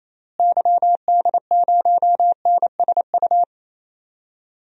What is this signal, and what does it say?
Morse code, heard as a keyed tone, about 28 wpm, 700 Hz. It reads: YD0NSU
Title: DXCC-100 Call Signs - 6 Characters - Repeated 3x - 28wpm